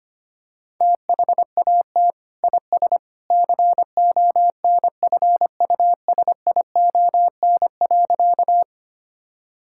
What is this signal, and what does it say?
Morse code, heard as a keyed tone, about 25 wpm, 700 Hz. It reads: THAT IS CONFUSION.
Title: Sentences from Top 1600 Words 25wpm